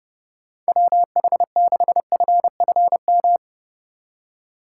Morse code, heard as a keyed tone, about 30 wpm, 700 Hz. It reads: WH6FFM
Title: USA Call Signs - 6 Characters - 4x Character Spacing - 30wpm